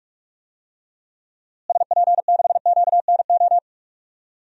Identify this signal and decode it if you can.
Morse code, heard as a keyed tone, about 46 wpm, 700 Hz. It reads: SP6QNO
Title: DXCC-100 Call Signs - 6 Characters - Repeated 3x - 45wpm